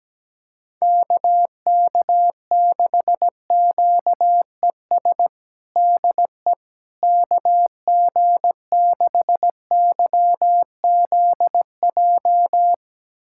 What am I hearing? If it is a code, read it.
Morse code, heard as a keyed tone, about 17 wpm, 700 Hz. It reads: KK6QES DE KG6YZJ
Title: USA Callsign DE Callsign - 6 Characters - 17wpm